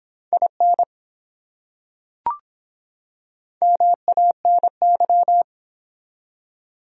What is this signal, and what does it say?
Morse code, heard as a keyed tone, about 24 wpm, 700 Hz. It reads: IN MANY